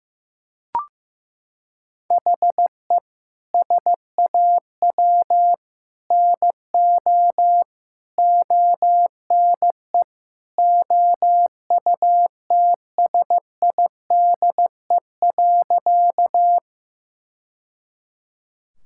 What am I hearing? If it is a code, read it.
Morse code, heard as a keyed tone, about 15 wpm, 700 Hz. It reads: HE SAW NO ONE OUTSIDE.
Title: Sentences from Top 1000 Words - Review Part 6 - 15wpm